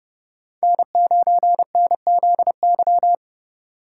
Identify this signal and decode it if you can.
Morse code, heard as a keyed tone, about 31 wpm, 700 Hz. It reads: N9NZY
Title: USA Call Signs - 5 Characters - 2x Character Spacing - 30wpm